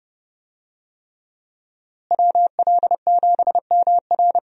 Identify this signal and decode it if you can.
Morse code, heard as a keyed tone, about 30 wpm, 700 Hz. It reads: WL7MR
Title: USA Call Signs - 5 Characters - Repeated 3x - 30wpm